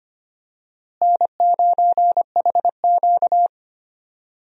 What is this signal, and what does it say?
Morse code, heard as a keyed tone, about 25 wpm, 700 Hz. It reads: N9HQ
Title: USA Call Signs - 4 Characters - Repeated 2x - 25wpm